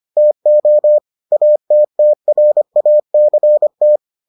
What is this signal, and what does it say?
Morse code, heard as a keyed tone, about 25 wpm, 600 Hz. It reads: TO ATTRACT